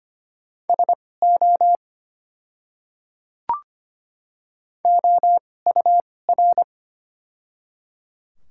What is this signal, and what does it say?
Morse code, heard as a keyed tone, about 25 wpm, spaced as for 18 wpm, 690 Hz. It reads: SO OUR
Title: 2x Character Spacing - Top 200 Words - 25wpm